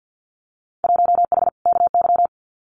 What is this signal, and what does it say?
Morse code, heard as a keyed tone, about 50 wpm, 700 Hz. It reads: 1H KY
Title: Winter Field Day Exchanges 50wpm